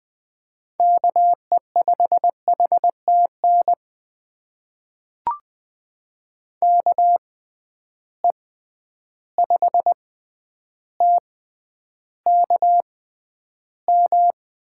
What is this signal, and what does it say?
Morse code, heard as a keyed tone, about 21 wpm, spaced as for 10 wpm, 700 Hz. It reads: KE5HTN KE5TKM
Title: USA Call Signs - 6 Characters - 6x Character Spacing - 20wpm